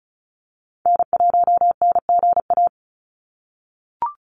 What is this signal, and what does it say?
Morse code, heard as a keyed tone, about 34 wpm, 700 Hz. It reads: N1NGA
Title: USA Call Signs 35wpm